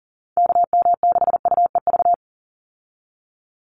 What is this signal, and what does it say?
Morse code, heard as a keyed tone, about 40 wpm, 700 Hz. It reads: KM6UEV
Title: USA Call Signs 40wpm